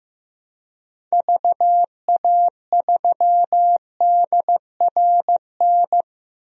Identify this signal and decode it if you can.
Morse code, heard as a keyed tone, about 15 wpm, 700 Hz. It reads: VA3DRN
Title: DXCC-100 Call Signs - 6 Characters - 15wpm